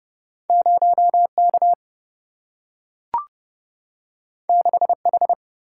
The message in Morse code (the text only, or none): 0K 6H